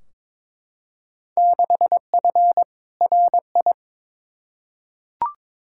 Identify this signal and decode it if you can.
Morse code, heard as a keyed tone, about 22 wpm, 700 Hz. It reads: 6F RI
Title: Field Day Exchanges Encore - 22wpm